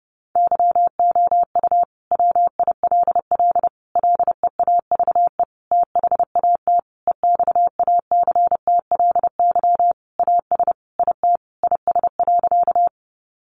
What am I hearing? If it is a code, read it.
Morse code, heard as a keyed tone, about 30 wpm, 700 Hz. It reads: YOU WILL LEAVE THAT EXACTLY AS IT IS.
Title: Sentences from Top 900 Words 30wpm